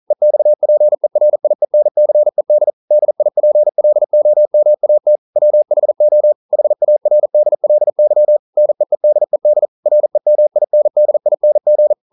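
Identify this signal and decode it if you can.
Morse code, heard as a keyed tone, about 40 wpm, 600 Hz. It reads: EXPERIENCED DIPLOMAT WHO HARDLY NEEDED REMINDING
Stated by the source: Morse code practice at 41 wpm, from CW med Gnister.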